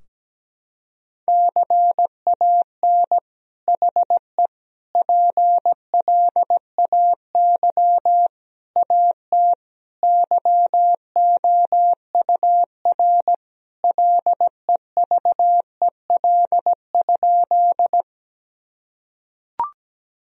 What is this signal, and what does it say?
Morse code, heard as a keyed tone, about 17 wpm, 700 Hz. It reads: CAN HE PLAY AT YOUR LEVEL?
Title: Sentences from Top 1000 Words - Review Part 4 - 17wpm